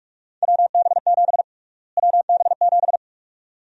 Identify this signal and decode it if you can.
Morse code, heard as a keyed tone, about 46 wpm, 700 Hz. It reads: WB7 WB7
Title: USA Call Sign Prefixes - Repeated 2x - 45wpm